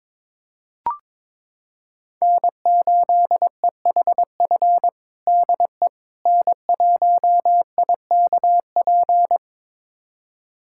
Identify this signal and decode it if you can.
Morse code, heard as a keyed tone, about 22 wpm, 700 Hz. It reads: N8EHF DE N1IKP
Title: USA Callsign DE Callsign - 5 Characters - 22wpm